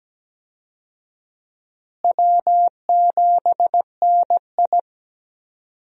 Morse code, heard as a keyed tone, about 17 wpm, 700 Hz. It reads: W7NI
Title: USA Call Signs - 4 Characters - Repeated 2x - 17wpm